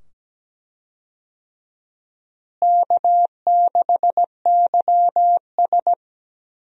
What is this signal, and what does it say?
Morse code, heard as a keyed tone, about 17 wpm, 700 Hz. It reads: K6YS
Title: USA Call Signs - 4 Characters - Repeated 2x - 17wpm